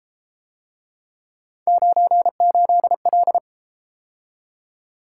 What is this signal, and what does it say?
Morse code, heard as a keyed tone, about 32 wpm, 700 Hz. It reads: 98L